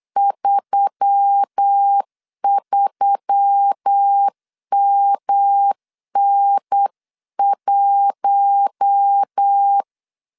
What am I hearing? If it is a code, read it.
Morse code, heard as a keyed tone, about 8 wpm, 800 Hz. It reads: 33MN1
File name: CW / DET / 1 6 1500C S B 8wpm 5142